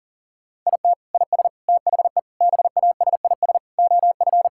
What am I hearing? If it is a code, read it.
Morse code, heard as a keyed tone, about 40 wpm, 700 Hz. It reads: IT IS THE BASIS OF
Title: Sentences from Top 1000 Words - Review Part 5 - 40wpm